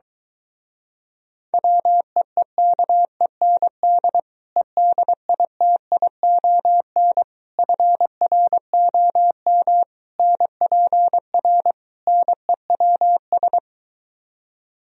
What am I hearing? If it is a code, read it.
Morse code, heard as a keyed tone, about 23 wpm, 700 Hz. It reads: WEEKEND EDITION FROM NPR NEWS